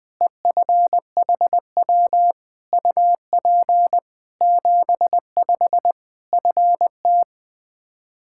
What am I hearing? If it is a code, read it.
Morse code, heard as a keyed tone, about 20 wpm, 700 Hz. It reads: EFHW UP 75 FT
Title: Practice QSO - Exchange 2 - Info - 20wpm